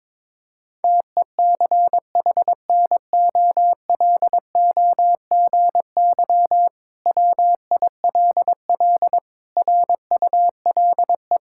Morse code, heard as a keyed tone, about 21 wpm, 700 Hz. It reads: TECHNOLOGY WILL RULE
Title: Sentences from Top 1000 Words - Review Part 5 - 22wpm